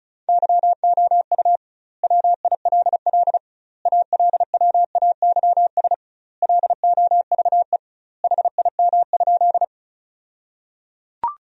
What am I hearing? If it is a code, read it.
Morse code, heard as a keyed tone, about 36 wpm, 700 Hz. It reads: YOU WILL ALWAYS LOVE HIM?
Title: 2x Word-Spacing - Sentences from Top 400 Words - 35wpm